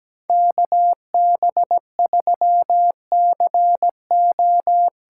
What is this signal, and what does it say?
Morse code, heard as a keyed tone, about 17 wpm, 700 Hz. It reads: KB3CO
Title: USA Call Signs - 6 Characters - 8x Character Spacing - 17wpm